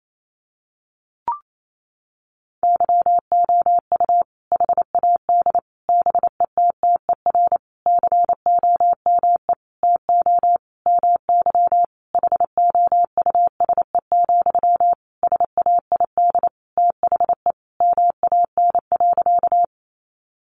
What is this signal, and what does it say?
Morse code, heard as a keyed tone, about 28 wpm, 700 Hz. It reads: YOU HAD BETTER COME TO MY HOUSE, SAID THE MAN.
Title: Sentences from Top 400 Words 28wpm